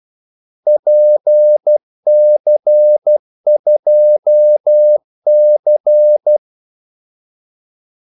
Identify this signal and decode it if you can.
Morse code, heard as a keyed tone, about 12 wpm, 600 Hz. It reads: PC2C
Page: 2 ekstra ordmellemrum mellem kaldesignalerne